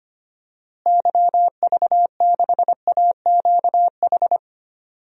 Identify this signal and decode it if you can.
Morse code, heard as a keyed tone, about 25 wpm, 700 Hz. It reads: YV6AQH
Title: DXCC-100 Call Signs - 6 Characters - Repeated 2x - 25wpm